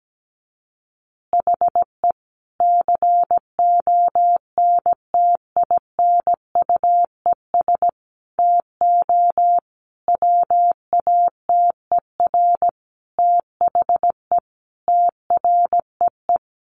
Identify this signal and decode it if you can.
Morse code, heard as a keyed tone, about 17 wpm, 700 Hz. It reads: HE CONTINUES TO WATER THE TREE
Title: Easy Phrases 17wpm